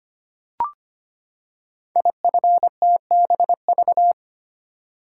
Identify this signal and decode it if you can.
Morse code, heard as a keyed tone, about 25 wpm, 700 Hz. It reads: IFTBV
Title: Code Groups - 5 Letters - 25wpm